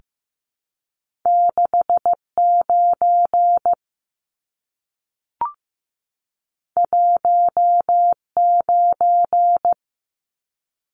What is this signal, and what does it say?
Morse code, heard as a keyed tone, about 15 wpm, 700 Hz. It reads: 69 19
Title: Numbers - 2-Digit 15wpm